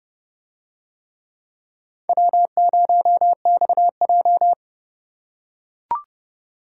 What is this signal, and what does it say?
Morse code, heard as a keyed tone, about 28 wpm, 700 Hz. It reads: W0XJ